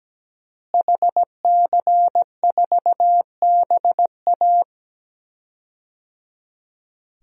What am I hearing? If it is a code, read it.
Morse code, heard as a keyed tone, about 17 wpm, 700 Hz. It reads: HC4BA
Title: DXCC-100 Call Signs - 5 Characters - Repeated 3x - 17wpm